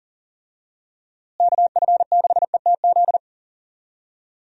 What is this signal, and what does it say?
Morse code, heard as a keyed tone, about 40 wpm, 700 Hz. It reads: KF6ETZ
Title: USA Call Signs - 6 Characters - Repeated 3x - 40wpm